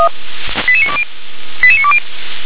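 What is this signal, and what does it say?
Five-tone selective call (ZVEI), heard as quick runs of five tones, tones of about 80 ms. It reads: 88920 88920
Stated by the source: ZVEI five-tone selective call 88920 received on an SDR.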